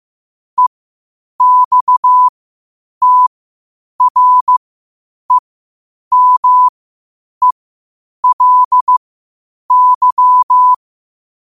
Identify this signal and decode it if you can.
Morse code, heard as a keyed tone, about 15 wpm, spaced as for 8 wpm, 1000 Hz. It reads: EXTREMELY
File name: qotdfiles / 8wpm